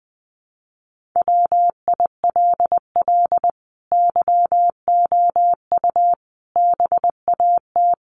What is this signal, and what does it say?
Morse code, heard as a keyed tone, about 20 wpm, 700 Hz. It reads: WILL YOU BAT
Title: Sentences from Top 1500 Words 20wpm